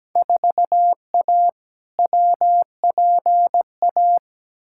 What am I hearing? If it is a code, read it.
Morse code, heard as a keyed tone, about 17 wpm, 700 Hz. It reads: 4A WPA
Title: Field Day Exchanges Encore - 17wpm